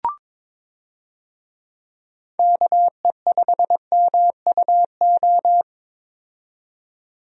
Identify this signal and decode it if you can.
Morse code, heard as a keyed tone, about 22 wpm, 700 Hz. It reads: KE5MUO
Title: USA Call Signs - 6 Characters - 22wpm